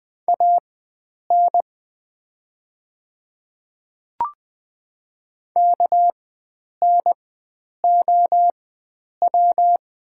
Morse code, heard as a keyed tone, about 20 wpm, spaced as for 9 wpm, 690 Hz. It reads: AN KNOW
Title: 4x Character Spacing - Top 100 Words - 20wpm